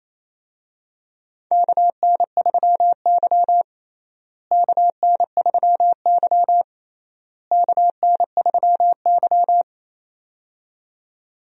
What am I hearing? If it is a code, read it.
Morse code, heard as a keyed tone, about 28 wpm, 700 Hz. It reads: KN3Y KN3Y KN3Y
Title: USA Call Signs - 4 Characters - Repeated 3x - 28wpm